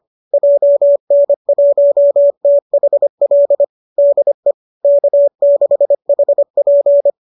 Morse code, heard as a keyed tone, about 25 wpm, 590 Hz. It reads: JN1THL DE K6HP
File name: qso205 25